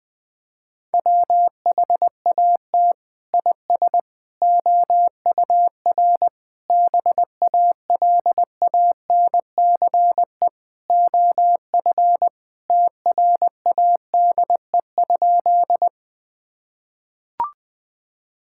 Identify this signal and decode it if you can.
Morse code, heard as a keyed tone, about 19 wpm, 700 Hz. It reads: WHAT IS OUR BALANCE OF TRADE?
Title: Sentences from Top 1000 Words - Review Part 6 - 20wpm